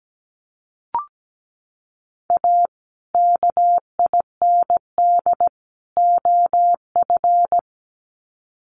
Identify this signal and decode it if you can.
Morse code, heard as a keyed tone, about 17 wpm, 700 Hz. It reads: A KIND OF
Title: Sets of 3 Words 17wpm